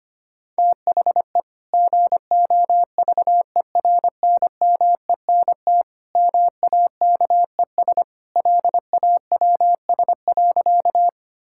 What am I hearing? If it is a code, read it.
Morse code, heard as a keyed tone, about 25 wpm, 700 Hz. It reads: THE GOVERNMENT MAKES LAWS.
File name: practice / 25wpm / Sentences with 2-Word Phrases - 25wpm